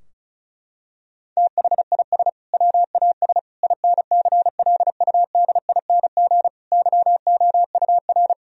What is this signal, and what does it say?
Morse code, heard as a keyed tone, about 36 wpm, 700 Hz. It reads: THIS WAS INCLUDING YOUR
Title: Sentences from Top 1000 Words - Review Part 4 - 35wpm